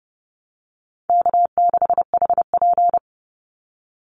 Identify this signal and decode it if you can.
Morse code, heard as a keyed tone, about 30 wpm, 700 Hz. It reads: K6HP
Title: Easy Call Signs - 30wpm